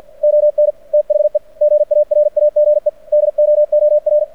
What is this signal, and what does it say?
Morse code, heard as a keyed tone, about 48 wpm, 600 Hz. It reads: ON THE GARAGE DOOR
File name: archived / CRSnet